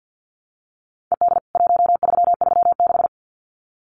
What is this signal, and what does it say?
Morse code, heard as a keyed tone, about 49 wpm, 700 Hz. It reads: ED 1336
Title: CWT Exchange 50wpm